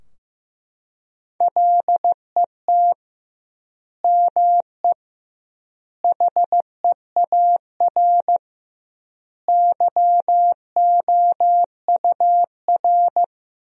Morse code, heard as a keyed tone, about 15 wpm, 700 Hz. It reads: LET ME HEAR YOUR
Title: 2x Word-Spacing - Sentences from Top 300 Words - 15wpm